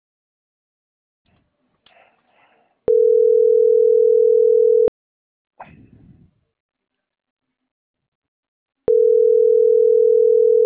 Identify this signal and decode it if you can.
Ringback tone (North America), 440 + 480 Hz.
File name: CENTRAL SOTOVA / out roberto.miranda